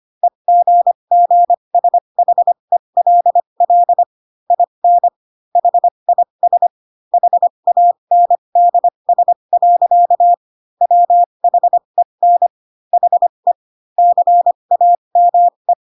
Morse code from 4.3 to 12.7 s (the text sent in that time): IN HIS HANDS. WHEN